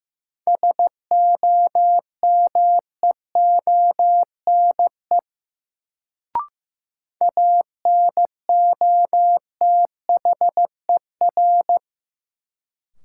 Morse code, heard as a keyed tone, about 15 wpm, 700 Hz. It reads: SOMEONE ANOTHER
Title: Sentences from Top 600 Words - New Word Drill - 15wpm